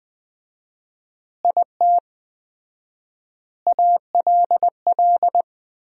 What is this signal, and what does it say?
Morse code, heard as a keyed tone, about 20 wpm, 700 Hz. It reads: IT ALL
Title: 4x Word-Spacing - Sentences from Top 200 Words - 20wpm